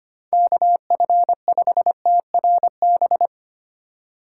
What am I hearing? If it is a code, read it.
Morse code, heard as a keyed tone, about 25 wpm, 700 Hz. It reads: KF5TRB
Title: USA Call Signs - 6 Characters - 3x Character Spacing - 25wpm